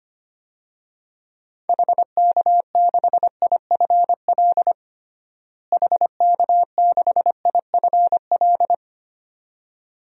Morse code, heard as a keyed tone, about 25 wpm, 700 Hz. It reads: HK6IFL HK6IFL
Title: DXCC-100 Call Signs - 6 Characters - Repeated 2x - 25wpm